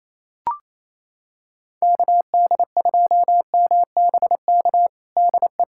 Morse code, heard as a keyed tone, about 28 wpm, 700 Hz. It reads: KD2MBK DE
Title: USA Callsign DE Callsign 28wpm